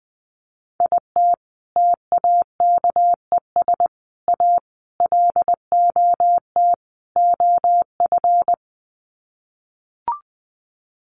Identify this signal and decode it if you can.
Morse code, heard as a keyed tone, about 20 wpm, 700 Hz. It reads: IT TAKES A LOT OF
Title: Sets of 5 Words Encore 20wpm